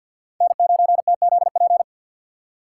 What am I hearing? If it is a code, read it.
Morse code, heard as a keyed tone, about 64 wpm, 700 Hz. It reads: N9TZP